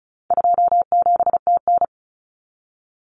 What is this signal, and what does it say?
Morse code, heard as a keyed tone, about 34 wpm, 700 Hz. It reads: 27TN
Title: Common Cut Numbers - 4-Digit 35wpm